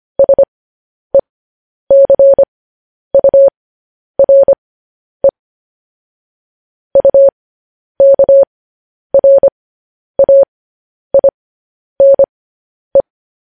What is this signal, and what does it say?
Morse code, heard as a keyed tone, about 25 wpm, spaced as for 10 wpm, 550 Hz. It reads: SECURE UKRAINE
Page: news-headlines-in-morse-code-at-10-wpm